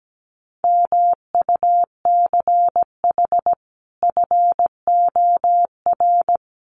Morse code, heard as a keyed tone, about 17 wpm, 700 Hz. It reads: MUCH FOR
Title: Sets of 5 Words Encore 17wpm